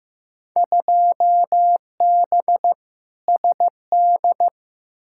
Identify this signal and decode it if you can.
Morse code, heard as a keyed tone, about 15 wpm, 700 Hz. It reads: 2B SD
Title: Field Day Exchanges Encore - 15wpm